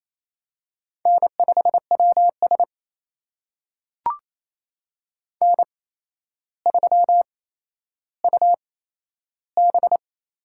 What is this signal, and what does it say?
Morse code, heard as a keyed tone, about 28 wpm, spaced as for 12 wpm, 700 Hz. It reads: N5WS N3UB
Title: USA Call Signs - 4 Characters - 8x Character Spacing - 28wpm